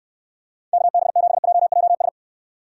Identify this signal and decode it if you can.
Morse code, heard as a keyed tone, about 69 wpm, 700 Hz. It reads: KB7JJS